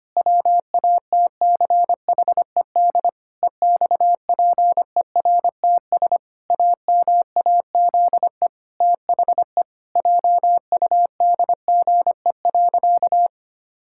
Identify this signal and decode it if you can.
Morse code, heard as a keyed tone, about 25 wpm, 700 Hz. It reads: WATCHED EXPERTS AMAZE THE JUDGE.
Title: Pangram Challenge v2 - 25wpm